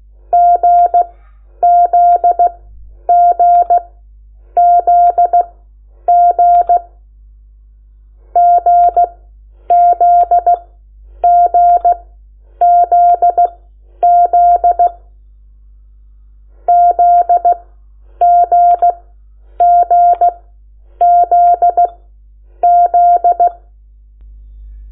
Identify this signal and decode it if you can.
Morse code, heard as a keyed tone, about 16 wpm, spaced as for 9 wpm, 700 Hz. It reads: GZGZG GZGZZ ZGGZZ
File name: Lecciones / CLASE Nº8